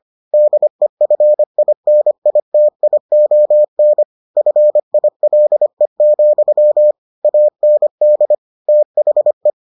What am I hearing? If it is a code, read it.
Morse code, heard as a keyed tone, about 25 wpm, 600 Hz. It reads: DEFINITION FILE, AND THE